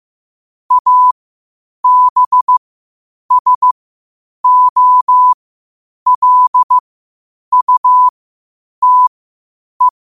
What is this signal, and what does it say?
Morse code, heard as a keyed tone, about 15 wpm, spaced as for 8 wpm, 1000 Hz. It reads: ABSOLUTE
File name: qotdfiles / 8wpm